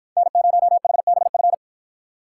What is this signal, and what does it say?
Morse code, heard as a keyed tone, about 53 wpm, 700 Hz. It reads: N0HBF